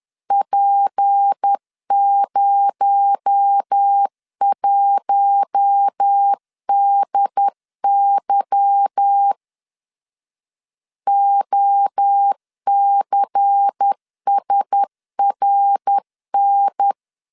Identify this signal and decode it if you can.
Morse code, heard as a keyed tone, about 10 wpm, 800 Hz. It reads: P01DY OCSRN
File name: LEZIONI / 8 YJ45 / 8 Y 10wpm 2615